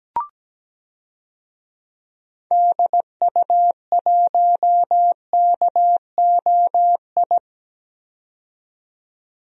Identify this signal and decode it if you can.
Morse code, heard as a keyed tone, about 17 wpm, 700 Hz. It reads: DU1KOI
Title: DXCC-100 Call Signs - 6 Characters - 17wpm